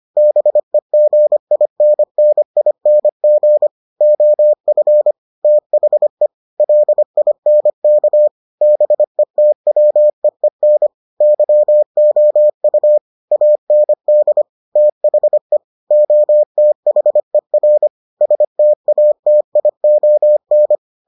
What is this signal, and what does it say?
Morse code, heard as a keyed tone, about 25 wpm, 600 Hz. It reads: BEGINNING OF THE LINK BETWEEN YOU AND THE OTHER STATION